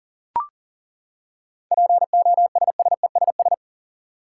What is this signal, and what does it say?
Morse code, heard as a keyed tone, about 40 wpm, 700 Hz. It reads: POSSESS
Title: Seven Letter Words - 40wpm